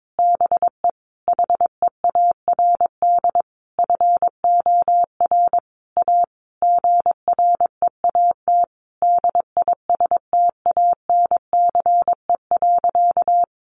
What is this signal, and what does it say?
Morse code, heard as a keyed tone, about 22 wpm, 700 Hz. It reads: BE HEARD FOR A GREAT DISTANCE.
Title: Sentences from Top 1200 Words 22wpm